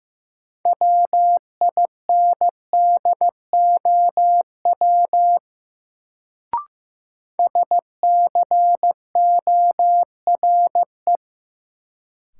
Morse code, heard as a keyed tone, about 15 wpm, 700 Hz. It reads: WINDOW SCORE
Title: Sentences from Top 1600 Words - New Word Drill - 15wpm